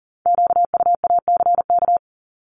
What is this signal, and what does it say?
Morse code, heard as a keyed tone, about 41 wpm, 700 Hz. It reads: QUACK
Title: Multi-letter Phonemes 40wpm